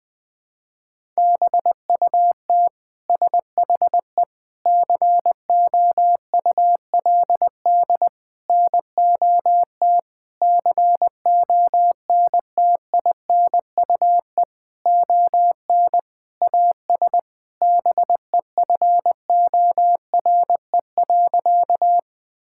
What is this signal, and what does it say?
Morse code, heard as a keyed tone, about 20 wpm, 700 Hz. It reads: BUT SHE COULD NOT CONTINUE ON AS BEFORE.
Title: Sentences from Top 1000 Words - Review Part 6 - 20wpm